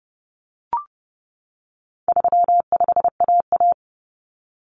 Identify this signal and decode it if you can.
Morse code, heard as a keyed tone, about 30 wpm, 700 Hz. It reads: 35AA
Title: Common Cut Numbers - 4-Digit 30wpm